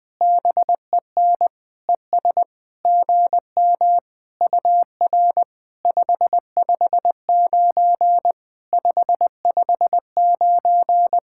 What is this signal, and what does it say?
Morse code, heard as a keyed tone, about 20 wpm, 700 Hz. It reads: BEN ES GM UR 559 559
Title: POTA QSO Practice - 20wpm